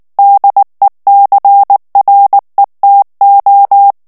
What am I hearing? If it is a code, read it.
Morse code, heard as a keyed tone, about 19 wpm, 800 Hz. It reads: DECRETO
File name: morse mp3 / ESPAGNOL TEXTE EN CLAIR 2 VITESSE 18 MOTS MINUTES